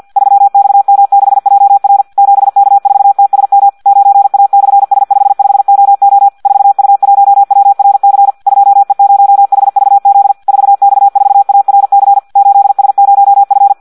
Morse code, heard as a keyed tone, about 50 wpm, 800 Hz. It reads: 3XM6JR 7K4TSM 9A/S55OO 4U1WRC 2E0HVZ 4X4ARC 8S0F
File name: Audio practica CW / qrq-tx-50 2